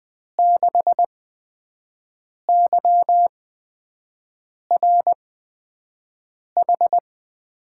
Morse code, heard as a keyed tone, about 20 wpm, spaced as for 5 wpm, 700 Hz. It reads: 6YRH
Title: USA Call Signs - 6 Characters - 8x Character Spacing - 20wpm